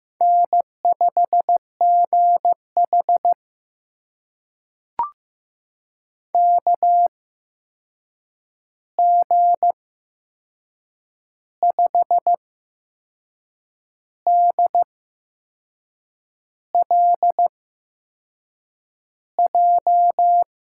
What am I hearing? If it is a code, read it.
Morse code, heard as a keyed tone, about 15 wpm, spaced as for 6 wpm, 700 Hz. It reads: N5GH KG5DLJ
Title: USA Call Signs - 8x Character Spacing - 15wpm